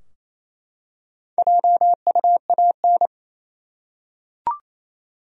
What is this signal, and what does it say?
Morse code, heard as a keyed tone, about 28 wpm, 700 Hz. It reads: JUAN
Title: USA Common Names - Max 4 Letters - 28wpm